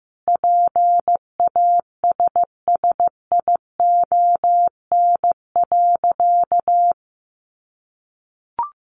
Morse code, heard as a keyed tone, about 15 wpm, 700 Hz. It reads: PASSION.
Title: Sentences from Top 1300 Words 15wpm